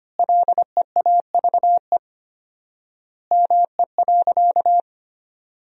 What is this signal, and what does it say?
Morse code, heard as a keyed tone, about 25 wpm, 700 Hz. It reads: LEAVE ME.
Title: 4x Word-Spacing - Sentences from Top 400 Words - 25wpm